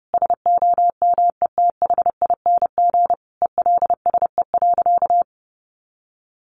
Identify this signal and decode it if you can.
Morse code, heard as a keyed tone, about 30 wpm, 700 Hz. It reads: SOMETHING ELSE.
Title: Sentences from Top 800 Words 30wpm